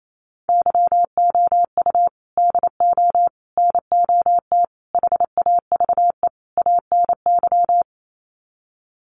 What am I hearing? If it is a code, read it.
Morse code, heard as a keyed tone, about 28 wpm, 700 Hz. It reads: YOU DO NOT HAVE ANY
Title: Sets of 5 Words Encore 28wpm